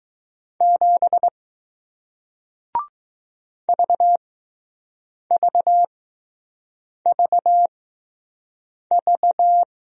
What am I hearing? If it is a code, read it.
Morse code, heard as a keyed tone, about 20 wpm, spaced as for 6 wpm, 700 Hz. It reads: 7 VVVV